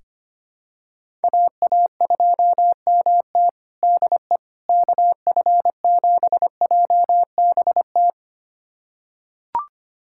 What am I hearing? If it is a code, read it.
Morse code, heard as a keyed tone, about 25 wpm, 700 Hz. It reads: AA2MT DE KF7JBT
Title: USA Callsign DE Callsign 25wpm